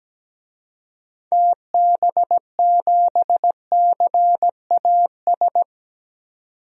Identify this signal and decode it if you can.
Morse code, heard as a keyed tone, about 17 wpm, 700 Hz. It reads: TB7CAS
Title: DXCC-100 Call Signs - 6 Characters - 17wpm